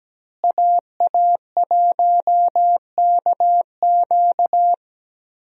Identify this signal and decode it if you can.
Morse code, heard as a keyed tone, about 17 wpm, 700 Hz. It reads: AA1KQ
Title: USA Call Signs - 5 Characters - Repeated 3x - 17wpm